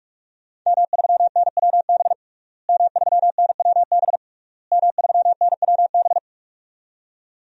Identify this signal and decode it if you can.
Morse code, heard as a keyed tone, about 43 wpm, 700 Hz. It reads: M3NWB M3NWB M3NWB
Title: DXCC-100 Call Signs - 5 Characters - Repeated 3x - 45wpm